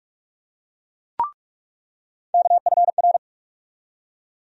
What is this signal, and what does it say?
Morse code, heard as a keyed tone, about 44 wpm, 700 Hz. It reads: KFR
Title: USA Call Sign Postfixes - 3 Letters 45wpm